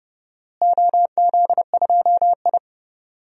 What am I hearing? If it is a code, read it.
Morse code, heard as a keyed tone, about 30 wpm, 700 Hz. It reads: OZ2I
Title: DXCC-100 Call Signs - 4 Characters - Repeated 3x - 30wpm